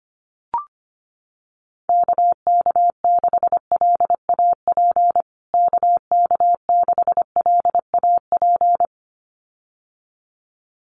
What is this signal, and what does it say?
Morse code, heard as a keyed tone, about 25 wpm, 700 Hz. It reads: KK6LAP KK6LAP
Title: USA Call Signs Repeated 2x 25wpm